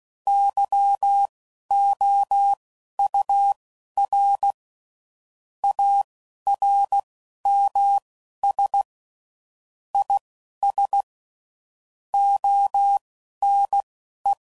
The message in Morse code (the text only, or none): YOUR ARMS IS ONE